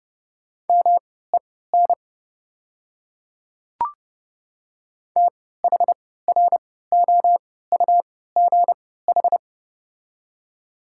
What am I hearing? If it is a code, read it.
Morse code, heard as a keyed tone, about 30 wpm, spaced as for 17 wpm, 700 Hz. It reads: MEN THROUGH
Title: 3x Character Spacing - Top 300 Words - 30wpm